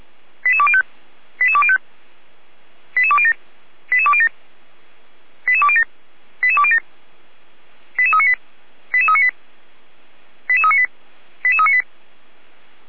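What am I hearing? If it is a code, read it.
Five-tone selective call (ZVEI), heard as quick runs of five tones, tones of about 70 ms. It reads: 88205 88205 88207 88207 88207 88207 88398 88398 88398 88398
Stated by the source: ZVEI five-tone call received on an SDR.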